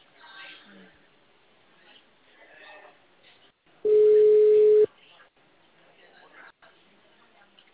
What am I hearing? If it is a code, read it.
Ringback tone (Europe, 425 Hz), with speech.
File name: out- -2030- - - .47